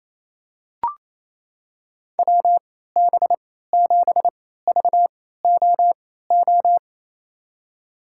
Morse code, heard as a keyed tone, about 28 wpm, spaced as for 16 wpm, 700 Hz. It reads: WB7VOO
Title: USA Call Signs - 6 Characters - 3x Character Spacing - 28wpm